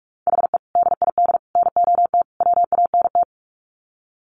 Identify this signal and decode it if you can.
Morse code, heard as a keyed tone, about 44 wpm, 700 Hz. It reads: HE DID NOT WANT
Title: Sets of 4 Words 45wpm